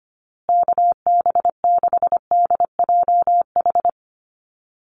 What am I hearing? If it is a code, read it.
Morse code, heard as a keyed tone, about 25 wpm, 700 Hz. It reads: KB6DJH
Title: USA Call Signs Repeated 3x 25wpm